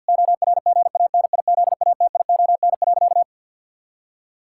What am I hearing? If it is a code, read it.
Morse code, heard as a keyed tone, about 50 wpm, 700 Hz. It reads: ORGANIZATION.
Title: Sentences from Top 1000 Words - Review Part 6 - 50wpm